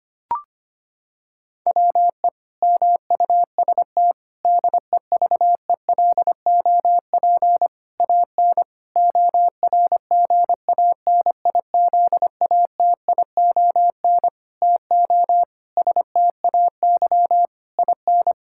Morse code, heard as a keyed tone, about 25 wpm, 700 Hz. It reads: WE MUST DEVELOP AN ORGANIZATION TO STAY IN
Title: Sentences from Top 1000 Words - Review Part 5 - 25wpm